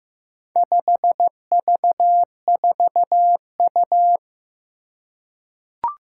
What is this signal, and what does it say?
Morse code, heard as a keyed tone, about 15 wpm, 700 Hz. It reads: 5V4U
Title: Four Challenging Letter-Number Combinations 15wpm